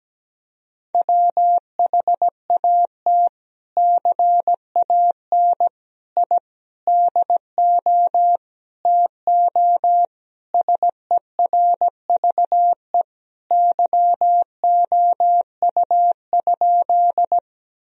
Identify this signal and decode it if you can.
Morse code, heard as a keyed tone, about 17 wpm, 700 Hz. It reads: WHAT CAN I DO TO SERVE YOU?
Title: Sentences from Top 1000 Words - Review Part 2 - 17wpm